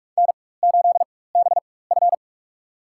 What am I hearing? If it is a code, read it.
Morse code, heard as a keyed tone, about 47 wpm, spaced as for 21 wpm, 700 Hz. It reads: N8BF
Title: USA Call Signs - 4 Characters - 4x Character Spacing - 45wpm